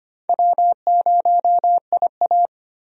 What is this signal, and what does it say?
Morse code, heard as a keyed tone, about 25 wpm, 700 Hz. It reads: W0IA
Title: USA Call Signs - 4 Characters - Repeated 3x - 25wpm